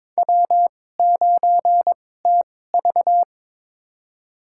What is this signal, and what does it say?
Morse code, heard as a keyed tone, about 22 wpm, spaced as for 16 wpm, 700 Hz. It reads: W9TV
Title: USA Call Signs - 4 Characters - 2x Character Spacing - 22wpm